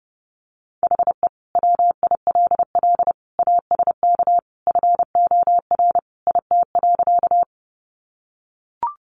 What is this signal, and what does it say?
Morse code, heard as a keyed tone, about 29 wpm, 700 Hz. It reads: HE WILL ASK FOR IT.
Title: Sentences from Top 200 Words 30wpm